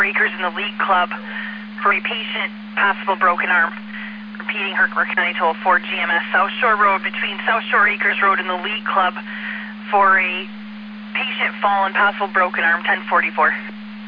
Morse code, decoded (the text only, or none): none